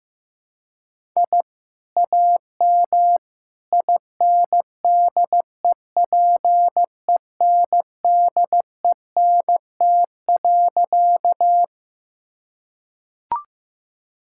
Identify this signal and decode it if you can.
Morse code, heard as a keyed tone, about 15 wpm, 700 Hz. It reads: I AM INDEPENDENT.
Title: Sentences from Top 1000 Words - Review Part 6 - 15wpm